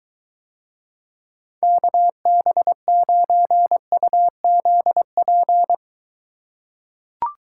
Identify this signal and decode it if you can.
Morse code, heard as a keyed tone, about 21 wpm, 700 Hz. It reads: KB9UZP